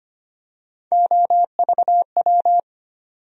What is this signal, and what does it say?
Morse code, heard as a keyed tone, about 25 wpm, 700 Hz. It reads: OVW
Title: USA Call Sign Postfixes - 3 Letters 25wpm